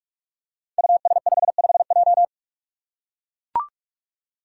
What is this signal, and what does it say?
Morse code, heard as a keyed tone, about 44 wpm, 700 Hz. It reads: US55J
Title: Five Challenging Letter-Number Combinations 45wpm